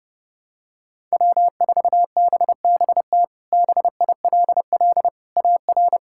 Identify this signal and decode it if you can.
Morse code, heard as a keyed tone, about 30 wpm, 700 Hz. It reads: W4BBT BILL AR
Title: K1USN SST QSO Practice - 30wpm